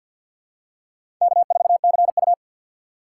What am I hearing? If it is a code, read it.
Morse code, heard as a keyed tone, about 53 wpm, 700 Hz. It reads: K4CU
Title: USA Call Signs - 4 Characters - 6x Character Spacing - 50wpm